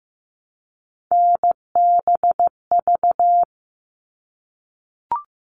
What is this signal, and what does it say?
Morse code, heard as a keyed tone, about 15 wpm, 700 Hz. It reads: NBV
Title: All Cut Numbers - 3-Digit 15wpm